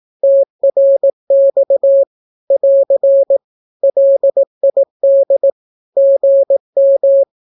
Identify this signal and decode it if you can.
Morse code, heard as a keyed tone, about 18 wpm, 550 Hz. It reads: TRX <AR> LID GM
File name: naukapodstaw / skroty / skroty